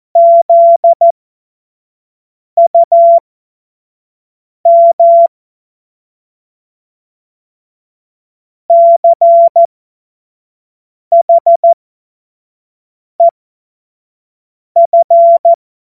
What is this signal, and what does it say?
Morse code, heard as a keyed tone, about 14 wpm, spaced as for 5 wpm, 680 Hz. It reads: ZUM CHEF
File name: Morsekurs / Witze / Witz 11. Juli 2023 25 F70